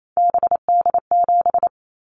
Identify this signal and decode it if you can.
Morse code, heard as a keyed tone, about 28 wpm, 700 Hz. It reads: BD7
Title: Callsign Prefixes for Top 100 DXCC - 28wpm